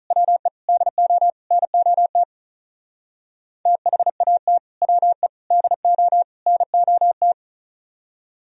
Morse code, heard as a keyed tone, about 45 wpm, 700 Hz. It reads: WE DO NOT THAT WE DO NOT